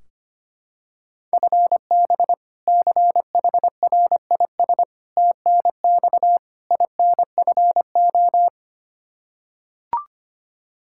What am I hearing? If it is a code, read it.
Morse code, heard as a keyed tone, about 25 wpm, 700 Hz. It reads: FB CHRIS TNX INFO
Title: Practice QSO - Exchange 3 - More Info - 25wpm